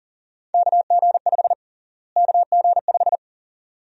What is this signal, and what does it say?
Morse code, heard as a keyed tone, about 40 wpm, 700 Hz. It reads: KG5 KG5
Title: USA Call Sign Prefixes - Repeated 2x - 40wpm